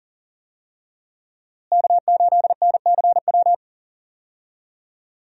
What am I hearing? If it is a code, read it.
Morse code, heard as a keyed tone, about 37 wpm, 700 Hz. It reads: K8NCW
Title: USA Call Signs - 5 Characters - Repeated 2x - 40wpm